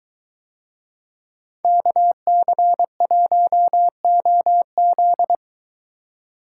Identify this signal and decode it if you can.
Morse code, heard as a keyed tone, about 23 wpm, 700 Hz. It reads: KC1OZ